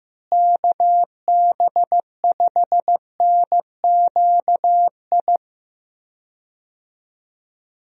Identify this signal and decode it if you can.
Morse code, heard as a keyed tone, about 15 wpm, 700 Hz. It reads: KB5NQI
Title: USA Call Signs - 6 Characters - Repeated 2x - 15wpm